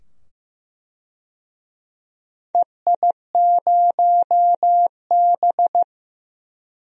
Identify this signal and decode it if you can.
Morse code, heard as a keyed tone, about 15 wpm, 700 Hz. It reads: EI0B
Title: DXCC-100 Call Signs - 4 Characters - 15wpm